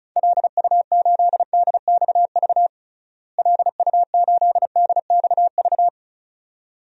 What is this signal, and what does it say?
Morse code, heard as a keyed tone, about 34 wpm, 700 Hz. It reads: LU8DXV LU8DXV
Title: DXCC-100 Call Signs - 6 Characters - Repeated 2x - 35wpm